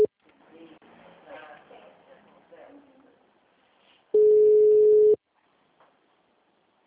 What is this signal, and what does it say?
Ringback tone (Europe, 425 Hz), with speech.